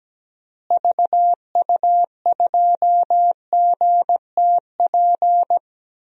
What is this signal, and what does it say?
Morse code, heard as a keyed tone, about 17 wpm, 700 Hz. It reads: VU2GTP
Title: DXCC-100 Call Signs - 6 Characters - Repeated 3x - 17wpm